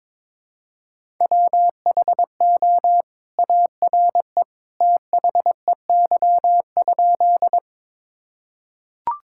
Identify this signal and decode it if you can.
Morse code, heard as a keyed tone, about 22 wpm, 700 Hz. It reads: WHO ARE THEY?
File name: practice / 22wpm / Sentences with 2-Word Phrases - 22wpm